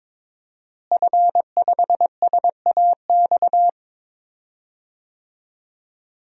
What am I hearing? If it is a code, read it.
Morse code, heard as a keyed tone, about 22 wpm, 700 Hz. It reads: F5SAX
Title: DXCC-100 Call Signs - 5 Characters - Repeated 3x - 22wpm